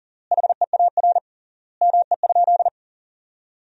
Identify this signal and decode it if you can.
Morse code, heard as a keyed tone, about 40 wpm, 700 Hz. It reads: HEAR ME?
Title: 3x Word-Spacing - Sentences from Top 300 Words - 40wpm